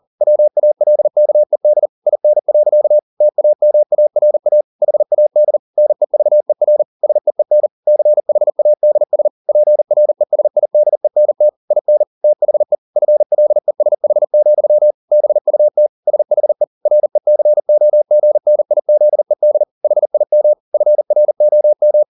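Morse code, heard as a keyed tone, about 40 wpm, 600 Hz. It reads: WALKED IN. TAMARA HAD NEVER SEEN CHADS PRESIDENT IN THE FLESH, BUT SHE RECOGNIZED HIM FROM
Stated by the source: Morse code at 40 wpm, practice text, CW med Gnister.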